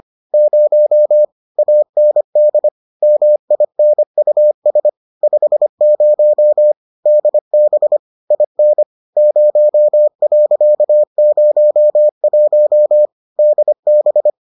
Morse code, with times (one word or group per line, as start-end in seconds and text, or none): 0.3-1.2 0
1.6-2.7 AND
3.0-4.9 MINUS
5.2-6.7 50
7.1-8.0 DB
8.3-8.8 IN
9.2-13.1 0.01
13.4-14.3 DB